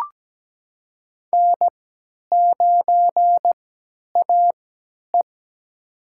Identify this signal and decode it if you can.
Morse code, heard as a keyed tone, about 17 wpm, spaced as for 10 wpm, 700 Hz. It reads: N9AE
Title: USA Call Signs - 4 Characters - 3x Character Spacing - 17wpm